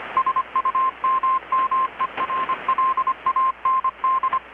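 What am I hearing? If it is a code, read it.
Morse code, heard as a keyed tone, about 25 wpm, 1050 Hz, with radio noise. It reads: SUMMERLAND
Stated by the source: Morse from KPH's Night of Nights XX maritime radiotelegraphy event, received in Chiapas, Mexico about 3,800 km away on a Kenwood R-600 with a grounded 10 m long-wire antenna.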